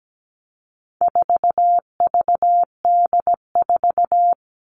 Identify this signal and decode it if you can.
Morse code, heard as a keyed tone, about 17 wpm, 700 Hz. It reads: 4VD4
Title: All Cut Numbers - 4-Digit 17wpm